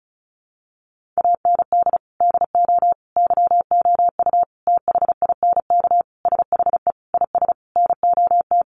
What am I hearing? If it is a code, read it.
Morse code, heard as a keyed tone, about 34 wpm, 700 Hz. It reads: AND DO YOU THINK SHE IS NOT
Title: Sentences from Top 600 Words 35wpm